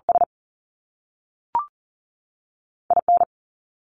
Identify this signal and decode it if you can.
Morse code, heard as a keyed tone, about 38 wpm, spaced as for 3 wpm, 700 Hz. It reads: SI N
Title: Top Two Letter Combinations in Words 40wpm